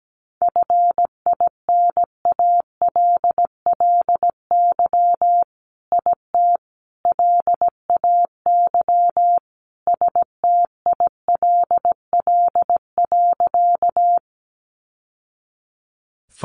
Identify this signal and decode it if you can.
Morse code, heard as a keyed tone, about 17 wpm, 700 Hz. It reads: FINALLY IT LAY STILL.
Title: Sentences from Top 1000 Words 17wpm